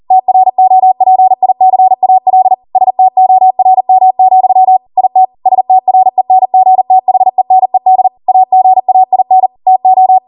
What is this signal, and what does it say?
Morse code, heard as a keyed tone, about 40 wpm, 750 Hz. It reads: TROPICAL STORM, IT STRENGTHENED AGAIN TO